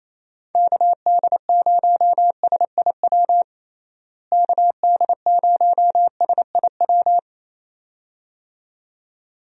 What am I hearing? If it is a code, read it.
Morse code, heard as a keyed tone, about 28 wpm, 700 Hz. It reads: KD0SIW KD0SIW
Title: USA Call Signs - 6 Characters - Repeated 2x - 28wpm